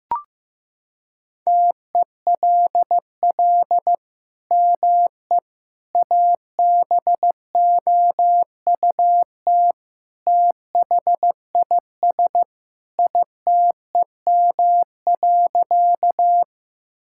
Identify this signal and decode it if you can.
Morse code, heard as a keyed tone, about 15 wpm, 700 Hz. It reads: TELL ME ABOUT THIS ITEM.
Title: Sentences from Top 1000 Words - Review Part 6 - 15wpm